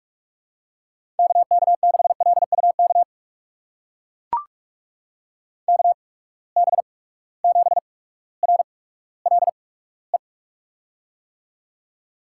Morse code, heard as a keyed tone, about 44 wpm, spaced as for 19 wpm, 700 Hz. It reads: KK6LUK KB7RLE
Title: USA Call Signs - 6 Characters - 8x Character Spacing - 45wpm